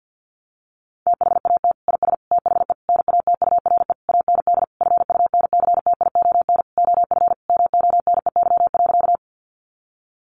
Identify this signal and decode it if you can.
Morse code, heard as a keyed tone, about 50 wpm, 700 Hz. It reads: THAT IS THE NATURE AND FUNCTION OF MONEY.
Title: Sentences from Top 1100 Words 50wpm